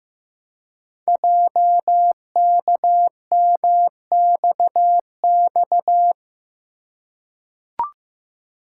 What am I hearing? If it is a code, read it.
Morse code, heard as a keyed tone, about 15 wpm, 700 Hz. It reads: JKMXX
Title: Code Groups - 5 Letters - 15wpm